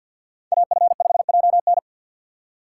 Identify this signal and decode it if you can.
Morse code, heard as a keyed tone, about 53 wpm, 700 Hz. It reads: AF5JN
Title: USA Call Signs - 5 Characters - 6x Character Spacing - 50wpm